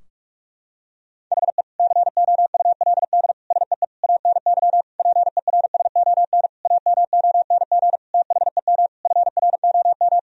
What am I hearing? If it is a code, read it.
Morse code, heard as a keyed tone, about 46 wpm, 700 Hz. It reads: HE COULD SEE ANY PERSON AMONG THEM FROM
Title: Sentences from Top 1000 Words - Review Part 3 - 45wpm